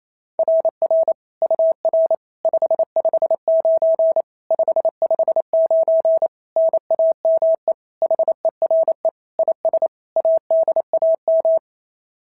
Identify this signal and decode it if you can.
Morse code, heard as a keyed tone, about 27 wpm, 650 Hz. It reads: RR UR 559 559 NAME HERE IS ADAM
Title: SOTA QSO Practice 28wpm